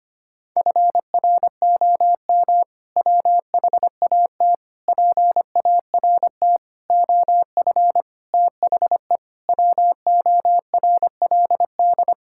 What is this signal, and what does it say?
Morse code, heard as a keyed tone, about 25 wpm, 700 Hz. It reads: FROM WHAT PART OF THE WORLD
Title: Sentences from Top 1000 Words - Review Part 6 - 25wpm